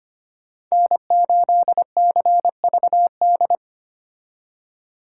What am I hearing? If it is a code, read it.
Morse code, heard as a keyed tone, about 32 wpm, 700 Hz. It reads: N8CVD